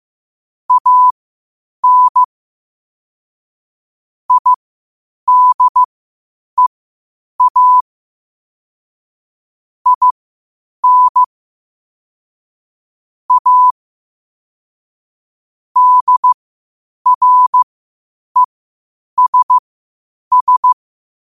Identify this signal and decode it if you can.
Morse code, heard as a keyed tone, about 15 wpm, spaced as for 8 wpm, 1000 Hz. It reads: AN IDEA IN A DRESS
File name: qotdfiles / 8wpm